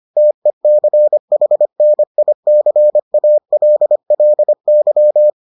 Morse code, heard as a keyed tone, about 25 wpm, 600 Hz. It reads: TECHNICALLY